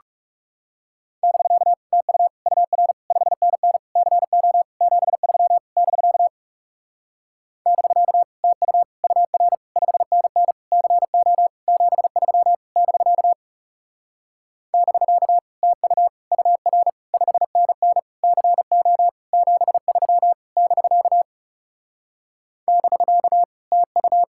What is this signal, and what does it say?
Morse code, heard as a keyed tone, about 40 wpm, 700 Hz. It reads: <BK> TU UR 5NN CO 73 <BK> <BK> TU UR 5NN CO 73 <BK> <BK> TU UR 5NN CO 73 <BK> <BK> TU